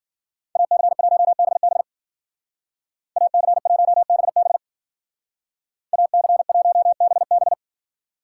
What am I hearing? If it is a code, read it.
Morse code, heard as a keyed tone, about 49 wpm, 700 Hz. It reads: AC1BB AC1BB AC1BB
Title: Speed Racing - USA 2x2 Call Signs 40wpm